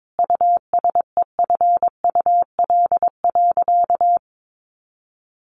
Morse code, heard as a keyed tone, about 22 wpm, 700 Hz. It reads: USEFUL.
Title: Sentences from Top 1500 Words 22wpm